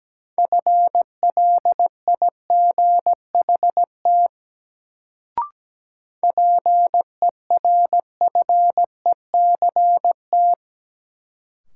Morse code, heard as a keyed tone, about 17 wpm, 700 Hz. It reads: FLIGHT PERFECT
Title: Sentences from Top 1000 Words - New Word Drill - 17wpm